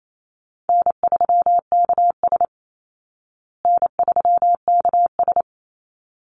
Morse code, heard as a keyed tone, about 28 wpm, 700 Hz. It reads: N3KS N3KS
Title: Easy Call Signs Repeated 2x - 28wpm